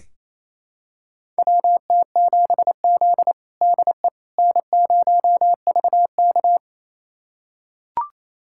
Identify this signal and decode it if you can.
Morse code, heard as a keyed tone, about 28 wpm, 700 Hz. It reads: WT7Z DE N0VK
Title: USA Callsign DE Callsign - 4 Characters - 28wpm